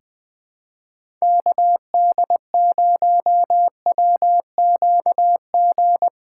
Morse code, heard as a keyed tone, about 20 wpm, 700 Hz. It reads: KD0WQG
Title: USA Call Signs - 6 Characters - 3x Character Spacing - 20wpm